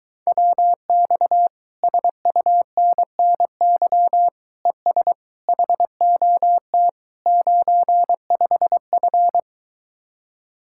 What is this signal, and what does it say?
Morse code, heard as a keyed tone, about 22 wpm, 700 Hz. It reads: WX SUNNY ES HOT 95F